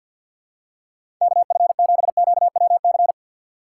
Morse code, heard as a keyed tone, about 49 wpm, 700 Hz. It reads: KF7QWC
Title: USA Call Signs - 3x Character Spacing - 50wpm